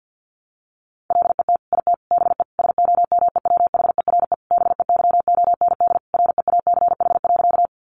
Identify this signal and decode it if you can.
Morse code, heard as a keyed tone, about 50 wpm, 700 Hz. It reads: LET IT BE SOMEWHERE BEYOND REACH.
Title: Sentences from Top 1400 Words 50wpm